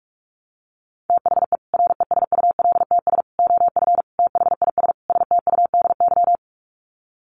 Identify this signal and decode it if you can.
Morse code, heard as a keyed tone, about 44 wpm, 700 Hz. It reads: THE RESULTS OF THIS STUDY
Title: Sets of 5 Words 45wpm